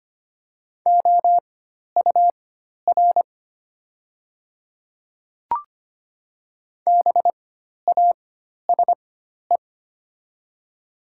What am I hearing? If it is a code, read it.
Morse code, heard as a keyed tone, about 25 wpm, spaced as for 12 wpm, 700 Hz. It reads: OUR BASE
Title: 4x Character Spacing - Top 500 Words - 25wpm